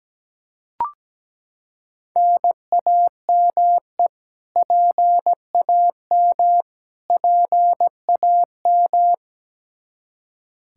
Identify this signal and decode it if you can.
Morse code, heard as a keyed tone, about 17 wpm, 700 Hz. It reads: NAME PAM PAM
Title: Practice QSO - Exchange 1 - Beginning - 17wpm